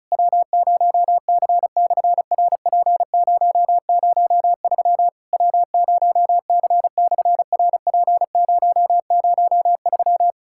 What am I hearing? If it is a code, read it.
Morse code, heard as a keyed tone, about 36 wpm, 700 Hz. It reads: W0C/RP003 W0C/RP003
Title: SOTA QSO Practice 35wpm